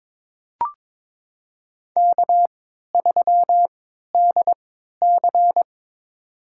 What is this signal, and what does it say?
Morse code, heard as a keyed tone, about 22 wpm, spaced as for 12 wpm, 700 Hz. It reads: K3DC
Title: USA Call Signs - 4 Characters - 3x Character Spacing - 22wpm